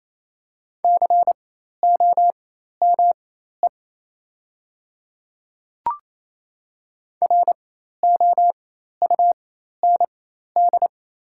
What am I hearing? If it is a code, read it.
Morse code, heard as a keyed tone, about 27 wpm, spaced as for 13 wpm, 690 Hz. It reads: COME ROUND
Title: 4x Character Spacing - Top 200 Words - 28wpm